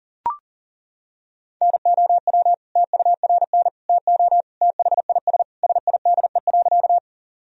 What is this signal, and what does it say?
Morse code, heard as a keyed tone, about 40 wpm, 700 Hz. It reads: NOW TURN TO THIS SIDE.
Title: Sentences from Top 1000 Words - Review Part 1 - 40wpm